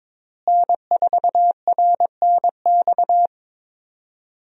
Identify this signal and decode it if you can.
Morse code, heard as a keyed tone, about 22 wpm, 700 Hz. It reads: N4RNX
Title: USA Call Signs - 5 Characters - Repeated 2x - 22wpm